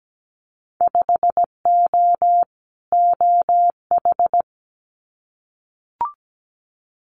Morse code, heard as a keyed tone, about 17 wpm, 700 Hz. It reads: 5O OH
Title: Winter Field Day Exchanges Encore 17wpm